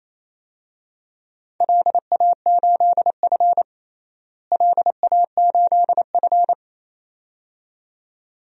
Morse code, heard as a keyed tone, about 28 wpm, 700 Hz. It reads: LA8F LA8F
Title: DXCC-100 Call Signs - 4 Characters - Repeated 2x - 28wpm